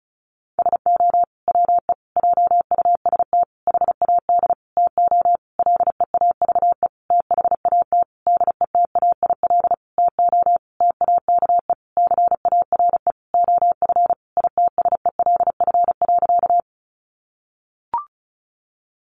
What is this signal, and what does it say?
Morse code, heard as a keyed tone, about 34 wpm, 700 Hz. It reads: SO WE JUST HAD TO LEAVE THAT DETAIL TO TAKE CARE OF ITSELF.
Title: Sentences from Top 1500 Words 35wpm